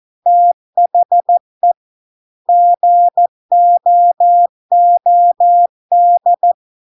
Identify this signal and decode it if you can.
Morse code, heard as a keyed tone, about 14 wpm, 700 Hz. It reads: THE GOOD